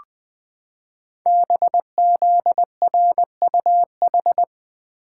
Morse code, heard as a keyed tone, about 20 wpm, 700 Hz. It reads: BZRUH
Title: Code Groups - 5 Letters - 20wpm